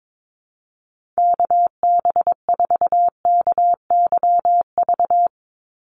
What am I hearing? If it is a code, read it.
Morse code, heard as a keyed tone, about 22 wpm, 700 Hz. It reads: KB4KYV
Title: USA Call Signs 22wpm